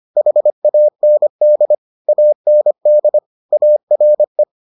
Morse code, heard as a keyed tone, about 25 wpm, 600 Hz. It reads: HAND AND ARE